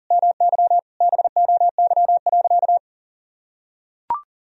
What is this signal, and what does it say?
Morse code, heard as a keyed tone, about 37 wpm, 700 Hz. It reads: MY BOY.
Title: Sentences from Top 1000 Words - Review Part 1 - 40wpm